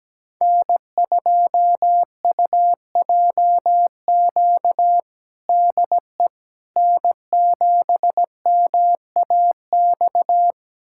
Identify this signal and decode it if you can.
Morse code, heard as a keyed tone, about 17 wpm, 700 Hz. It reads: N2UJQ DE N7MAX
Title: USA Callsign DE Callsign - 5 Characters - 17wpm